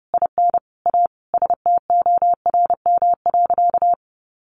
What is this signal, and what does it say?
Morse code, heard as a keyed tone, about 30 wpm, 700 Hz. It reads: IN A STORM.
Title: Sentences from Top 1500 Words 30wpm